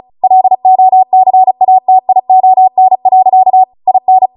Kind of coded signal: Morse code